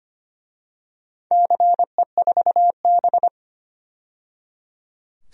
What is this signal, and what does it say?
Morse code, heard as a keyed tone, about 25 wpm, 700 Hz. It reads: CE4B
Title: DXCC-100 Call Signs - 4 Characters - 25wpm